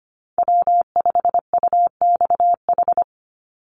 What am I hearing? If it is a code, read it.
Morse code, heard as a keyed tone, about 25 wpm, 700 Hz. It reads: W5UXH
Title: Call Signs Repeated 2x - 25wpm